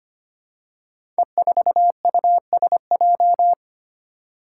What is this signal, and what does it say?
Morse code, heard as a keyed tone, about 25 wpm, 700 Hz. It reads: E4USJ
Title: Five Challenging Letter-Number Combinations 25wpm